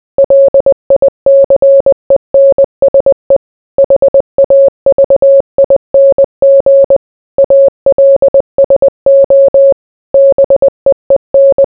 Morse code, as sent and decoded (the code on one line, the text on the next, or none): .-.. .. -.-. -. ... / .... .- ...- .. -. --. / .- .-.. ... --- / -... . -.
LICENSE HAVING ALSO BEEN